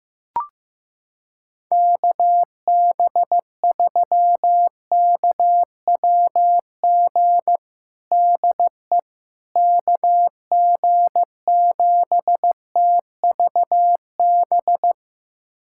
Morse code, heard as a keyed tone, about 15 wpm, 700 Hz. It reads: KB3KWG DE KG7TVB
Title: USA Callsign DE Callsign 15wpm